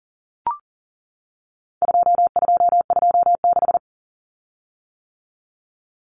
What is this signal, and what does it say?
Morse code, heard as a keyed tone, about 40 wpm, 700 Hz. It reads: 2226
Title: Common Cut Numbers - 4-Digit 40wpm